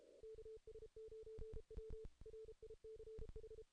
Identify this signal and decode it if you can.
Morse code, heard as a keyed tone, about 32 wpm, 440 Hz. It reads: KS9W RICH
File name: 2016 JAN CW NAQP / 2016 Jan naqp k5ot - 0-8